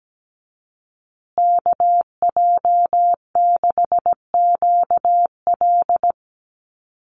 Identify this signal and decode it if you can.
Morse code, heard as a keyed tone, about 17 wpm, 700 Hz. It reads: KJ6QL
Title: USA Call Signs Repeated 3x 17wpm